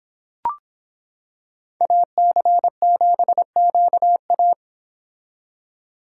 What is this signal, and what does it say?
Morse code, heard as a keyed tone, about 26 wpm, 700 Hz. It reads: AC7QA